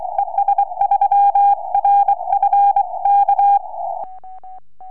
Morse code, heard as a keyed tone, about 23 wpm, 800 Hz. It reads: ES3RF K
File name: es3rf 4m